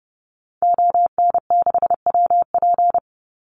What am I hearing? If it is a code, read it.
Morse code, heard as a keyed tone, about 30 wpm, 700 Hz. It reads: ON6WP
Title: Call Signs - 30wpm